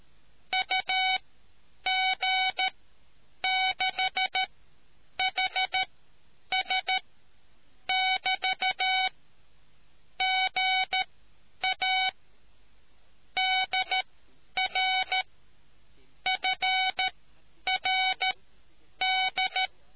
Morse code, heard as a keyed tone, about 13 wpm, spaced as for 8 wpm, 750 Hz. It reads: UG6HS <BT> GA DR FRD